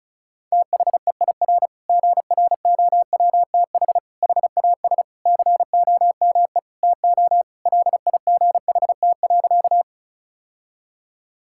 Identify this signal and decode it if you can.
Morse code, heard as a keyed tone, about 36 wpm, 700 Hz. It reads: THEIR GROWTH HAS COME TO LIGHT.
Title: Sentences from Top 1000 Words - Review Part 5 - 35wpm